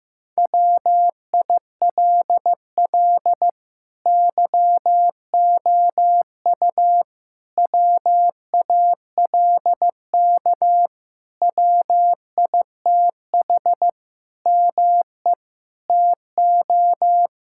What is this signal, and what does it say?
Morse code, heard as a keyed tone, about 15 wpm, 700 Hz. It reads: WILL YOU WALK WITH ME TO
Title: Sentences from Top 1000 Words - Review Part 6 - 15wpm